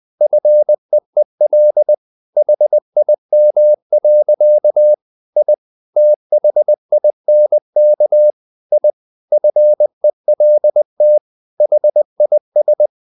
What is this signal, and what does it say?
Morse code, heard as a keyed tone, about 20 wpm, 600 Hz. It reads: FEEL HIM. I THINK I FELT HIS
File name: the old man and the sea part 2 20wpm